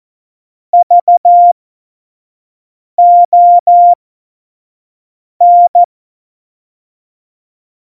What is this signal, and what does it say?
Morse code, heard as a keyed tone, about 14 wpm, spaced as for 5 wpm, 700 Hz. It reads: VON
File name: Morsekurs / Witze / Witz 07. Mai 2019, 25 F70